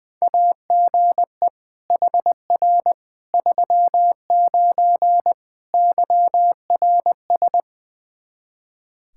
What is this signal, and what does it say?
Morse code, heard as a keyed tone, about 20 wpm, 700 Hz. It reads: AGE HR 39 YRS
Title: Practice QSO - Exchange 3 - More Info - 20wpm